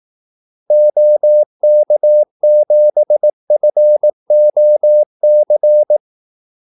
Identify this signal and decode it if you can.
Morse code, heard as a keyed tone, about 18 wpm, 600 Hz. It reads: OK7FOC
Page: Kaldesignaler 18 wpm